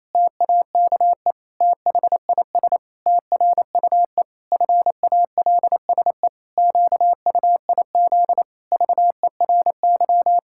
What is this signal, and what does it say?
Morse code, heard as a keyed tone, about 28 wpm, 700 Hz. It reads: TAKE THIS TRUE FALSE QUIZ VERY
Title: Pangram Challenge v2 - 28wpm